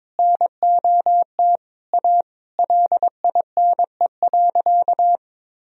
Morse code, heard as a keyed tone, about 22 wpm, 700 Hz. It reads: NOT A LINE.
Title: Sentences from Top 1000 Words - Review Part 1 - 22wpm